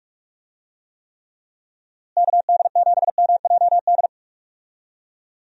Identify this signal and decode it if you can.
Morse code, heard as a keyed tone, about 41 wpm, 700 Hz. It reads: KD7MJD
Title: USA Call Signs - 6 Characters - Repeated 3x - 45wpm